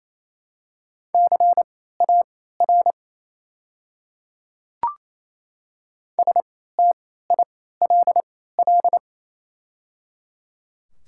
Morse code, heard as a keyed tone, about 28 wpm, spaced as for 16 wpm, 700 Hz. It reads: CAR STILL
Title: 3x Character Spacing - Top 300 Words - 28wpm